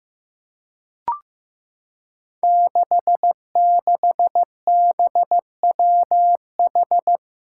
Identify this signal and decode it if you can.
Morse code, heard as a keyed tone, about 15 wpm, 700 Hz. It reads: 66BWH
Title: Five Challenging Letter-Number Combinations 15wpm